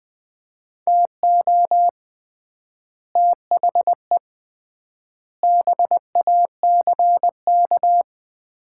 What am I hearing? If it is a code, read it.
Morse code, heard as a keyed tone, about 20 wpm, 700 Hz. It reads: TO THE BACK
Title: 3x Word-Spacing - Sets of 5 Words - 20wpm